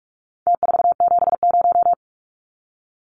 Morse code, heard as a keyed tone, about 44 wpm, 700 Hz. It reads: T470
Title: Numbers - Contest Style - 4-Digit 45wpm